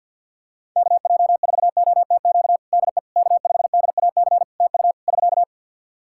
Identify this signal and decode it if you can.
Morse code, heard as a keyed tone, about 50 wpm, 700 Hz. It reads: KJ4OTQ DE K5DAC TU <SK>
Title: Practice QSO - Exchange 4 - Ending - 50wpm